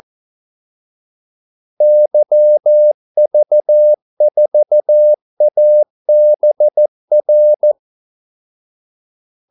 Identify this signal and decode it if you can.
Morse code, heard as a keyed tone, about 14 wpm, 600 Hz. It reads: YV4ABR
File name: Callsigns-14wpm